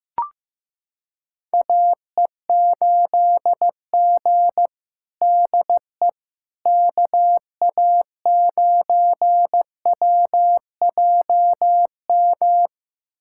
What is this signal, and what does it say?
Morse code, heard as a keyed tone, about 15 wpm, 700 Hz. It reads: AE8G DE KA9WJM
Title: USA Callsign DE Callsign 15wpm